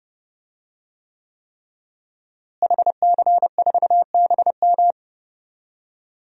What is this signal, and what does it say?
Morse code, heard as a keyed tone, about 30 wpm, 700 Hz. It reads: HC4BM
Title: DXCC-100 Call Signs - 5 Characters - Repeated 3x - 30wpm